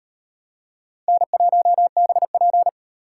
Morse code, heard as a keyed tone, about 38 wpm, 700 Hz. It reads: N1BP